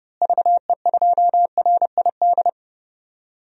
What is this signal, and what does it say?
Morse code, heard as a keyed tone, about 31 wpm, 700 Hz. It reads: VE2RID
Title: DXCC-100 Call Signs - 6 Characters - Repeated 3x - 30wpm